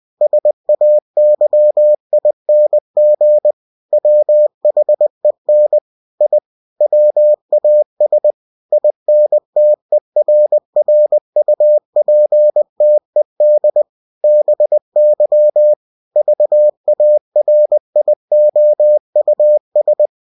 Morse code, heard as a keyed tone, about 20 wpm, 600 Hz. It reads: SAYING WHEN I WAS INTERRUPTED BY VARIOUS
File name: cw-books / winnie-the-pooh / winnie-20wpm-page-0108